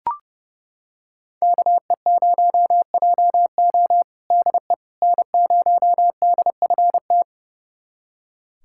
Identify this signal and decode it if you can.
Morse code, heard as a keyed tone, about 30 wpm, 700 Hz. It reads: KE0JO DE N0DFT
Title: USA Callsign DE Callsign - 5 Characters - 30wpm